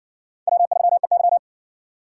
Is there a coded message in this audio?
Yes: Morse code from 0.5 to 1.4 s, heard as a keyed tone, about 60 wpm, 700 Hz. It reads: W3EY